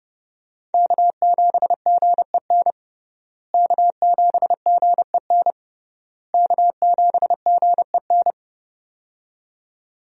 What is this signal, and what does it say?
Morse code, heard as a keyed tone, about 30 wpm, 700 Hz. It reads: K7GEN K7GEN K7GEN
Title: USA Call Signs - 5 Characters - Repeated 3x - 30wpm